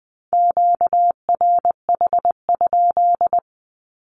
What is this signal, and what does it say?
Morse code, heard as a keyed tone, about 20 wpm, 700 Hz. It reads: QRH?
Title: Q-Codes Repeated 3x 20wpm